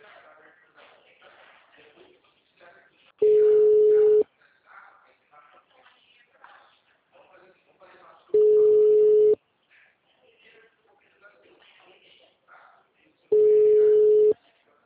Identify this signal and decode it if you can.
Ringback tone (Europe, 425 Hz), with speech.
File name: out- -2004- - - .34490